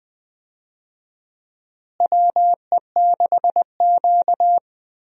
Morse code, heard as a keyed tone, about 20 wpm, 700 Hz. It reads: WE6Q
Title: USA Call Signs - 4 Characters - Repeated 2x - 20wpm